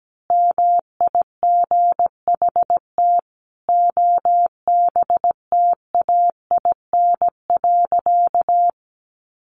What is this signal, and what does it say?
Morse code, heard as a keyed tone, about 17 wpm, 700 Hz. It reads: MIGHT OBTAIN.
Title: Sentences from Top 1300 Words 17wpm